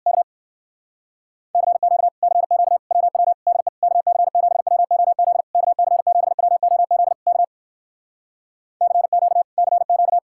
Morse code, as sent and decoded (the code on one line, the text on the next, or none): -.- / -.-. --.- / -.-. --.- / .-- .-- / -.. / -.- -.-. --... .-- --- --.. / -.- -.-. --... .-- --- --.. / -.- / -.-. --.- / -.-. --.-
K CQ CQ WW DE KC7WOZ KC7WOZ K CQ CQ